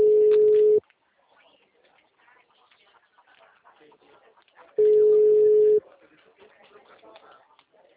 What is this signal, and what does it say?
Ringback tone (Europe, 425 Hz), with speech.